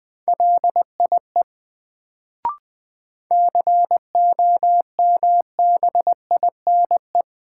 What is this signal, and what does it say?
Morse code, heard as a keyed tone, about 21 wpm, 700 Hz. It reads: LIE COMBINE
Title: Sentences from Top 1600 Words - New Word Drill Alt - 20wpm